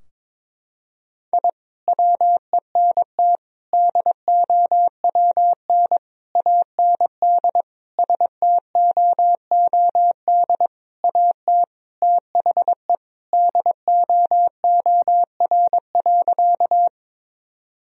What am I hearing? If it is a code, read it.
Morse code, heard as a keyed tone, about 22 wpm, 700 Hz. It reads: I WENT DOWN AND STOOD AT THE DOOR.
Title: Sentences from Top 1000 Words - Review Part 2 - 22wpm